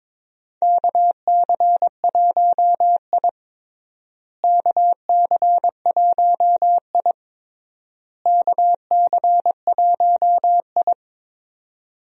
Morse code, heard as a keyed tone, about 21 wpm, 700 Hz. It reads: KC1I KC1I KC1I
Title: USA Call Signs - 4 Characters - Repeated 3x - 22wpm